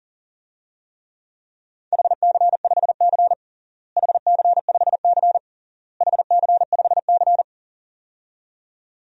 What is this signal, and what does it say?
Morse code, heard as a keyed tone, about 40 wpm, 700 Hz. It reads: HC5C HC5C HC5C
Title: DXCC-100 Call Signs - 4 Characters - Repeated 3x - 40wpm